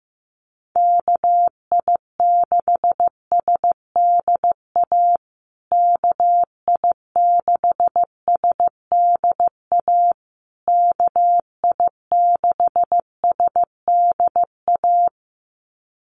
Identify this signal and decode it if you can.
Morse code, heard as a keyed tone, about 15 wpm, 700 Hz. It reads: KI6SDA KI6SDA KI6SDA
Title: USA Call Signs Repeated 3x 15wpm